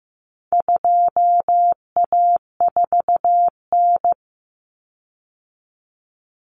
Morse code, heard as a keyed tone, about 15 wpm, 700 Hz. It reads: 2A4N
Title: Common Cut Numbers - 4-Digit 15wpm